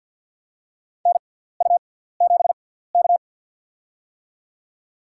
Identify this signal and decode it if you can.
Morse code, heard as a keyed tone, about 49 wpm, spaced as for 17 wpm, 700 Hz. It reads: NU7K
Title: USA Call Signs - 4 Characters - 6x Character Spacing - 50wpm